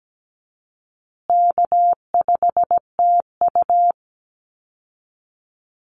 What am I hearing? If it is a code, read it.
Morse code, heard as a keyed tone, about 17 wpm, 700 Hz. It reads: K5TU
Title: Call Signs Repeated 3x - 17wpm